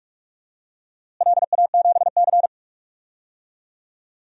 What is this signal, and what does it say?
Morse code, heard as a keyed tone, about 46 wpm, 700 Hz. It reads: LA7C
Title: DXCC-100 Call Signs - 4 Characters - Repeated 2x - 45wpm